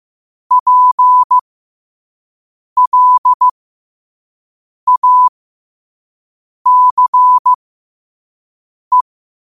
Morse code, heard as a keyed tone, about 15 wpm, spaced as for 5 wpm, 1000 Hz. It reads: PLACE
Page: qotd-at-5-wpm